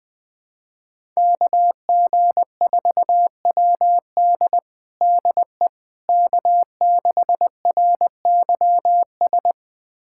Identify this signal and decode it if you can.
Morse code, heard as a keyed tone, about 21 wpm, 700 Hz. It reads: KG4WD DE K6RYS
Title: USA Callsign DE Callsign - 5 Characters - 20wpm